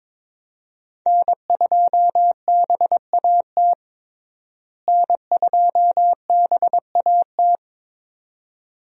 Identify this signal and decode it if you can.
Morse code, heard as a keyed tone, about 22 wpm, 700 Hz. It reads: N2BAT N2BAT
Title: USA Call Signs - 5 Characters - Repeated 2x - 22wpm